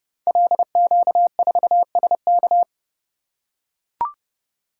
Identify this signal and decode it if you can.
Morse code, heard as a keyed tone, about 30 wpm, 700 Hz. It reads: LQ4SK
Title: Code Groups - 5 Letter:Numbers - 30wpm